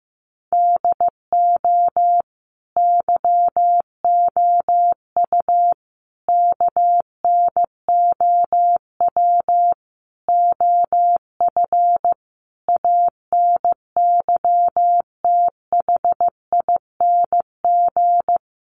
Morse code, heard as a keyed tone, about 15 wpm, 700 Hz. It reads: DO YOU KNOW OF ANYTHING
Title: Sentences from Top 1200 Words 15wpm